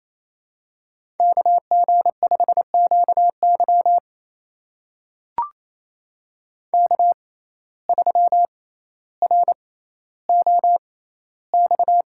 Morse code, heard as a keyed tone, about 27 wpm, spaced as for 14 wpm, 700 Hz. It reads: KG5QY K3ROX
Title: USA Call Signs - 5 Characters - 6x Character Spacing - 28wpm